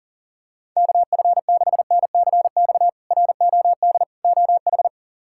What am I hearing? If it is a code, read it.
Morse code, heard as a keyed tone, about 40 wpm, 700 Hz. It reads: KF6NCX ROD OH
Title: K1USN SST QSO Practice - 40wpm